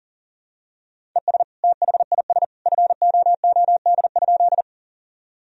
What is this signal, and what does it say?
Morse code, heard as a keyed tone, about 40 wpm, 700 Hz. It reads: ES THIS FOOD?
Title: Sentences from Top 300 Words and Abbreviations 40wpm